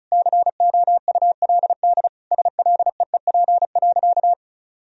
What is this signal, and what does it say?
Morse code, heard as a keyed tone, about 36 wpm, 700 Hz. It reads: COULD SLEEP.
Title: Sentences from Top 500 Words and Abbreviations 35wpm